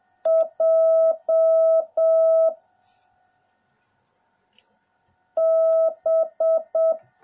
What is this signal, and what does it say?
Morse code, heard as a keyed tone, about 7 wpm, 650 Hz. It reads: J B